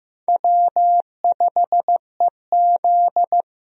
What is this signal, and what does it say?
Morse code, heard as a keyed tone, about 14 wpm, 700 Hz. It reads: W5EZ
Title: USA Call Signs - 4 Characters - Repeated 3x - 15wpm